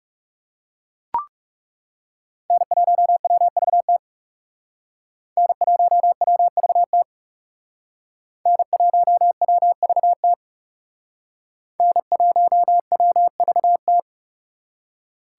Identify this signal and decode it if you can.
Morse code, heard as a keyed tone, about 37 wpm, 700 Hz. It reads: N1WVT N1WVT N1WVT N1WVT
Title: Speed Racing - USA 1x3 Call Signs 30wpm